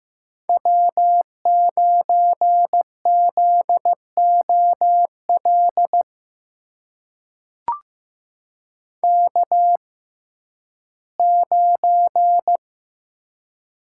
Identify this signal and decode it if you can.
Morse code, heard as a keyed tone, about 15 wpm, spaced as for 11 wpm, 700 Hz. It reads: W9ZOL K9
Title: USA Call Signs - 5 Characters - 6x Character Spacing - 15wpm